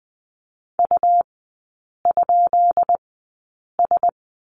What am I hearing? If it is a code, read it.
Morse code, heard as a keyed tone, about 20 wpm, spaced as for 8 wpm, 700 Hz. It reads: U?S
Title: 133 Lesson 22e - Question Mark - Three Characters 20wpm